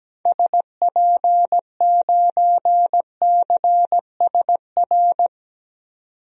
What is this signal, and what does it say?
Morse code, heard as a keyed tone, about 17 wpm, 700 Hz. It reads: SP9CSR
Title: DXCC-100 Call Signs - 6 Characters - 17wpm